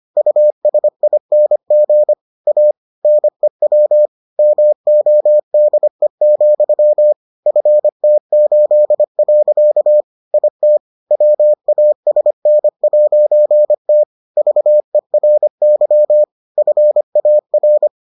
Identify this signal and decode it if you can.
Morse code, heard as a keyed tone, about 25 wpm, 600 Hz. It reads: USING A NEW MODE, FT8. IT WASN'T VERY FAR